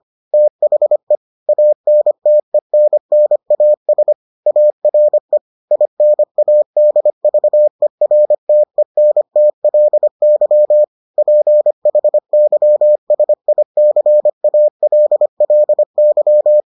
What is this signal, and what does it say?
Morse code, heard as a keyed tone, about 25 wpm, 600 Hz. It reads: THE ANTENNAS ARE INADVERTENTLY PHYSICALLY